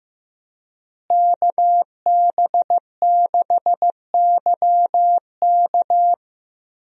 Morse code, heard as a keyed tone, about 15 wpm, 700 Hz. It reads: KB6YK
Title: USA Call Signs - 5 Characters - Repeated 3x - 15wpm